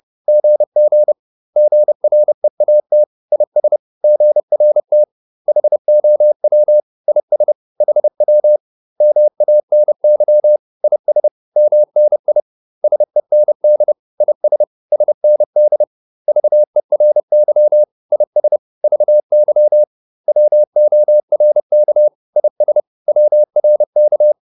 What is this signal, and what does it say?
Morse code, heard as a keyed tone, about 30 wpm, 600 Hz. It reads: GG GREAT IS GRT HOW IS HW MANY IS MNI SEND IS SND VERY IS VY WORK IS WRK